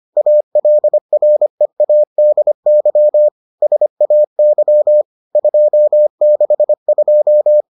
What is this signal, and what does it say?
Morse code, heard as a keyed tone, about 25 wpm, 600 Hz. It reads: ALREADY SAY 262